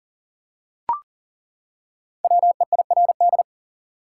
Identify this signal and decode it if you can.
Morse code, heard as a keyed tone, about 40 wpm, 700 Hz. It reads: WEIRD
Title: Five Letter Words - 40wpm